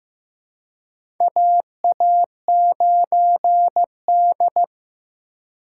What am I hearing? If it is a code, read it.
Morse code, heard as a keyed tone, about 15 wpm, 700 Hz. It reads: AA9D
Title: USA Call Signs - 4 Characters - 2x Character Spacing - 15wpm